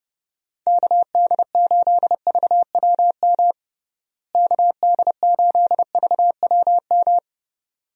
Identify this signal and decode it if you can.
Morse code, heard as a keyed tone, about 30 wpm, 700 Hz. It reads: KD8VWM KD8VWM
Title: USA Call Signs - 6 Characters - Repeated 2x - 30wpm